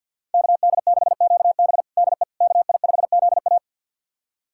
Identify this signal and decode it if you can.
Morse code, heard as a keyed tone, about 52 wpm, 700 Hz. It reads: KD6QB DE KI5ZA
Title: USA Callsign DE Callsign - 5 Characters - 50wpm